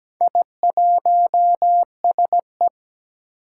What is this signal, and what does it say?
Morse code, heard as a keyed tone, about 17 wpm, 700 Hz. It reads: I1SE
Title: Four Challenging Letter-Number Combinations 17wpm